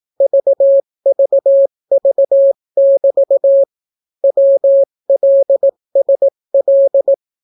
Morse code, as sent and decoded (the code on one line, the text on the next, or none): ...- ...- ...- -...- / .-- .-.. ... .-..
VVV<BT> WLSL